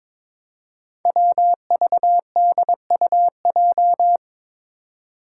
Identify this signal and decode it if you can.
Morse code, heard as a keyed tone, about 22 wpm, 700 Hz. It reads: WVDUJ
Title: Five Challenging Letter-Number Combinations 22wpm